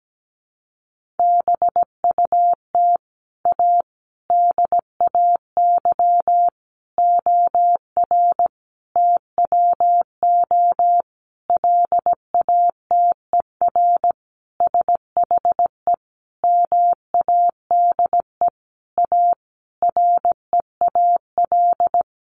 Morse code, heard as a keyed tone, about 17 wpm, 700 Hz. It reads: BUT A DAY OR TWO LATER SHE MADE A REAL
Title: Sentences from Top 1100 Words 17wpm